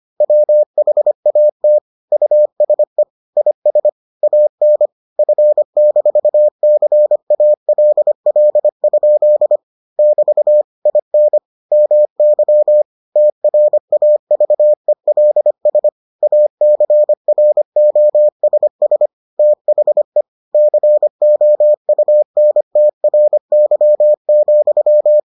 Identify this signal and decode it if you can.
Morse code, heard as a keyed tone, about 25 wpm, 600 Hz. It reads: WHAT USE IS AN F-CALL? <BT> IN MY TRAVELS ACROSS THE COUNTRY,